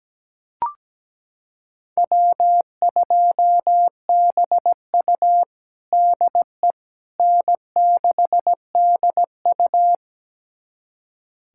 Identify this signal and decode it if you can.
Morse code, heard as a keyed tone, about 17 wpm, 700 Hz. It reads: W2BU DE N6DU
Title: USA Callsign DE Callsign - 4 Characters - 17wpm